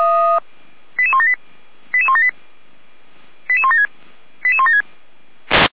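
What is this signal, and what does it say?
Five-tone selective call (ZVEI), heard as quick runs of five tones, tones of about 70 ms. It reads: 88178 88178 88176 88176
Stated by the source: ZVEI five-tone call received by SDR.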